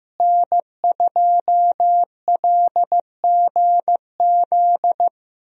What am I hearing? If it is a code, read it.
Morse code, heard as a keyed tone, about 15 wpm, 700 Hz. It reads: N2LGZ
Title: USA Call Signs - 5 Characters - Repeated 3x - 15wpm